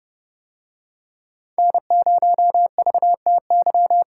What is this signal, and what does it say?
Morse code, heard as a keyed tone, about 29 wpm, 700 Hz. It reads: N0VTY